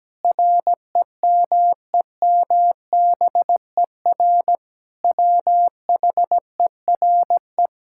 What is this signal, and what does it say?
Morse code, heard as a keyed tone, about 17 wpm, 700 Hz. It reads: REMEMBER WHERE
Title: Sentences from Top 1000 Words - Review Part 6 - 17wpm